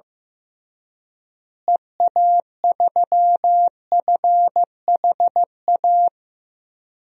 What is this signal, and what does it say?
Morse code, heard as a keyed tone, about 15 wpm, 700 Hz. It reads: EA3FHA
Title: DXCC-100 Call Signs - 6 Characters - Repeated 3x - 15wpm